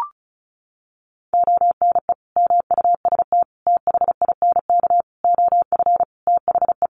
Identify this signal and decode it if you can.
Morse code, heard as a keyed tone, about 34 wpm, 700 Hz. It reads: ONE MUST THINK OF THE
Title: Sentences from Top 700 Words 35wpm